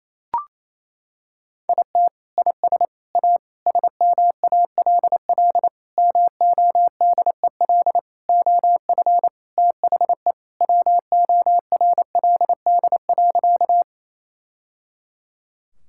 Morse code, heard as a keyed tone, about 28 wpm, 700 Hz. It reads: IT IS A SMALL MODEL OF THE WORLD.
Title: Sentences from Top 1000 Words - Review Part 5 - 28wpm